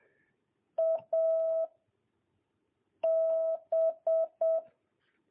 Morse code, heard as a keyed tone, about 7 wpm, 1300 Hz. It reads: A B
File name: jh8cbh / 2022cw